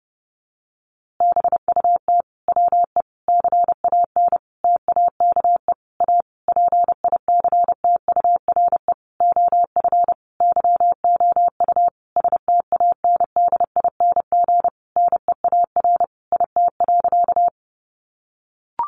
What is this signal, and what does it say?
Morse code, heard as a keyed tone, about 30 wpm, 700 Hz. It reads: BUT WE CAN TAKE A PICTURE OF YOU STANDING NEAR IT.
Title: A Sound Of Thunder 30wpm